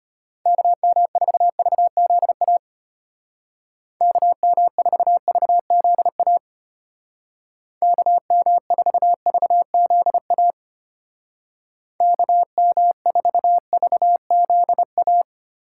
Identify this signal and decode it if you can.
Morse code, heard as a keyed tone, about 33 wpm, 700 Hz. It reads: KM4VZA KM4VZA KM4VZA KM4VZA